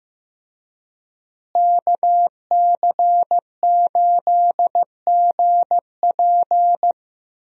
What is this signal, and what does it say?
Morse code, heard as a keyed tone, about 15 wpm, 700 Hz. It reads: KC8GP
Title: USA Call Signs - 5 Characters - Repeated 3x - 15wpm